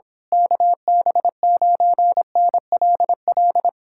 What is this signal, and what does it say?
Morse code, heard as a keyed tone, about 26 wpm, 700 Hz. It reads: KB9NLL
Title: Speed-Racing Common QSO Phrases - 22wpm